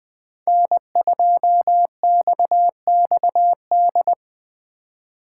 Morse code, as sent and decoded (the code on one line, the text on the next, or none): -. ..--- -..- -..- -..
N2XXD